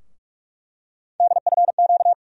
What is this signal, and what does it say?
Morse code, heard as a keyed tone, about 42 wpm, 700 Hz. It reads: DFQ
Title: USA Call Sign Postfixes - 3 Letters 45wpm